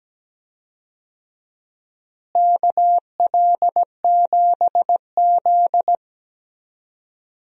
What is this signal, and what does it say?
Morse code, heard as a keyed tone, about 17 wpm, 700 Hz. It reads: KL7Z
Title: DXCC-100 Call Signs - 4 Characters - Repeated 2x - 17wpm